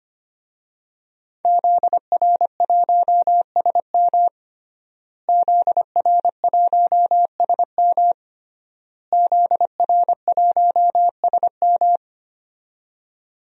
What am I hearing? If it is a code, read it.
Morse code, heard as a keyed tone, about 25 wpm, 700 Hz. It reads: ZR1SM ZR1SM ZR1SM
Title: DXCC-100 Call Signs - 5 Characters - Repeated 3x - 25wpm